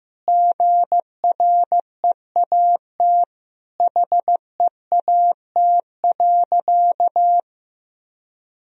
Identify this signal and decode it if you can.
Morse code, heard as a keyed tone, about 15 wpm, 700 Hz. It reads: GREAT HEAT.
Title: Sentences from Top 1000 Words - Review Part 3 - 15wpm